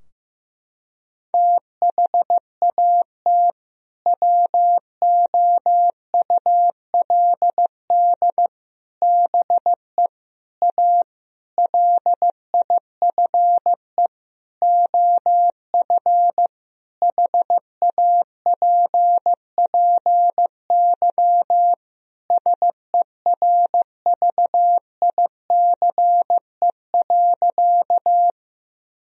Morse code, heard as a keyed tone, about 16 wpm, 700 Hz. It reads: THAT WOULD BE A LIFE OF HAPPY SERVICE.
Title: Sentences from Top 1000 Words - Review Part 4 - 15wpm